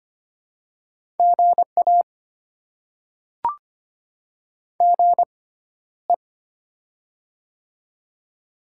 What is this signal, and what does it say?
Morse code, heard as a keyed tone, about 25 wpm, spaced as for 13 wpm, 700 Hz. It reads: GA GE
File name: Top 20 QSO Elements - 6x Character Spacing - 25wpm